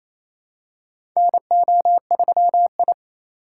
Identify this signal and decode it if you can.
Morse code, heard as a keyed tone, about 28 wpm, 700 Hz. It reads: NO3I
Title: USA Call Signs - 4 Characters - 4x Character Spacing - 28wpm